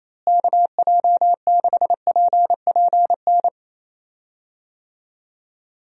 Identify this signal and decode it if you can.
Morse code, heard as a keyed tone, about 28 wpm, 700 Hz. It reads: KJ6PPN
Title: USA Call Signs - 6 Characters - Repeated 2x - 28wpm